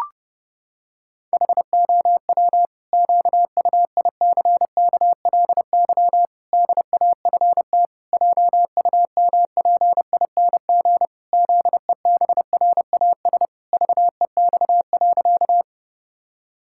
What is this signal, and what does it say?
Morse code, heard as a keyed tone, about 30 wpm, 700 Hz. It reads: HOW QUICKLY DAFT JUMPING ZEBRAS VEX.
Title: Pangram Challenge v2 - 30wpm